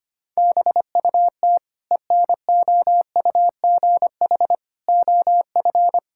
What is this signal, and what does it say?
Morse code, heard as a keyed tone, about 25 wpm, 700 Hz. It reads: BUT ENOUGH OF
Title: Sentences from Top 1000 Words - Review Part 6 - 25wpm